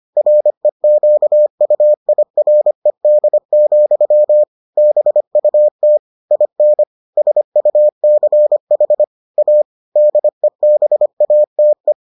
Morse code, heard as a keyed tone, about 25 wpm, 600 Hz. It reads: REQUIRED, BUT IN SUCH A DEBATE